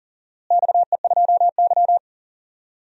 Morse code, heard as a keyed tone, about 39 wpm, 700 Hz. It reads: XE2Y
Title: DXCC-100 Call Signs - 4 Characters - Repeated 2x - 40wpm